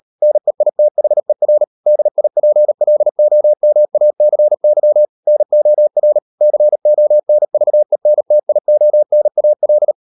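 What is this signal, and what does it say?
Morse code, heard as a keyed tone, about 38 wpm, 600 Hz. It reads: NEITHER DIPLOMACY NOR CONVENTIONAL
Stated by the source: Morse code practice at 35–39 wpm sending English text of Ken Follett's novel Never, from CW med Gnister.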